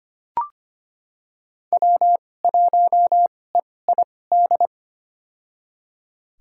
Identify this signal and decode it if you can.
Morse code, heard as a keyed tone, about 25 wpm, spaced as for 18 wpm, 700 Hz. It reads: W1EID
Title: USA Call Signs - 5 Characters - 2x Character Spacing - 25wpm